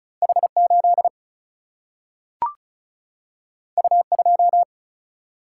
Morse code, heard as a keyed tone, about 36 wpm, 700 Hz. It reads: H8 U2
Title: Single-Letter Single-Number 35wpm